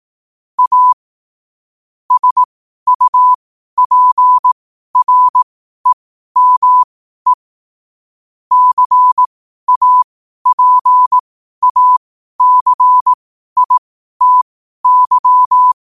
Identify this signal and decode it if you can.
Morse code, heard as a keyed tone, about 18 wpm, spaced as for 12 wpm, 1000 Hz. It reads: A SUPREME CAPACITY
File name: qotdfiles / 12wpm